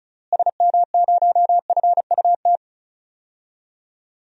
Morse code, heard as a keyed tone, about 34 wpm, 700 Hz. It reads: SM0FUT
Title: DXCC-100 Call Signs - 6 Characters - Repeated 2x - 35wpm